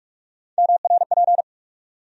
Morse code, heard as a keyed tone, about 46 wpm, 700 Hz. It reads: MRP
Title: USA Call Sign Postfixes - 3 Letters 45wpm